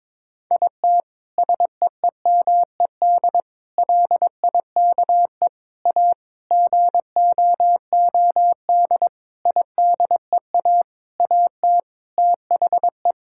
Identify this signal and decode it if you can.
Morse code, heard as a keyed tone, about 22 wpm, 700 Hz. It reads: IT SEEMED LIKE A GOOD IDEA AT THE
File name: Sets of 9 Words - 22wpm